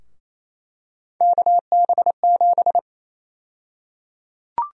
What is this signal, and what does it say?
Morse code, heard as a keyed tone, about 28 wpm, 700 Hz. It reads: KB7
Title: USA Call Sign Prefixes - Repeated 2x - 28wpm